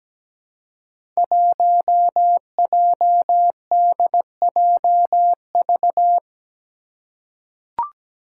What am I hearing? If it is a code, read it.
Morse code, heard as a keyed tone, about 17 wpm, 700 Hz. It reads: 1JDJV
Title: Five Challenging Letter-Number Combinations 17wpm